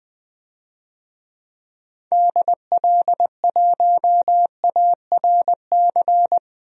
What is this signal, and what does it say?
Morse code, heard as a keyed tone, about 20 wpm, 700 Hz. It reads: DL1ARC
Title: DXCC-100 Call Signs - 6 Characters - 20wpm